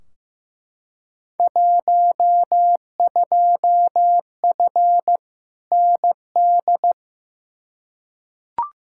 Morse code, heard as a keyed tone, about 15 wpm, 700 Hz. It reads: 12F ND
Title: Field Day Exchanges Encore - 15wpm